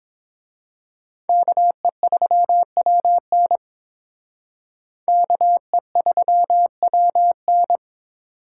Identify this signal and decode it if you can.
Morse code, heard as a keyed tone, about 26 wpm, 700 Hz. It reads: KE3WN KE3WN
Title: Speed Racing - USA 2x2 Call Signs 22wpm